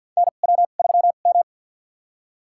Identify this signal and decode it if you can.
Morse code, heard as a keyed tone, about 50 wpm, spaced as for 36 wpm, 700 Hz. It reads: NW3M
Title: USA Call Signs - 4 Characters - 2x Character Spacing - 50wpm